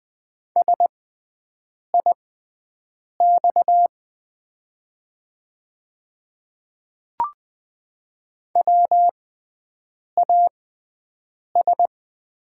Morse code, heard as a keyed tone, about 20 wpm, spaced as for 7 wpm, 700 Hz. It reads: SIX WAS
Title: 6x Character Spacing - Top 500 Words - 20wpm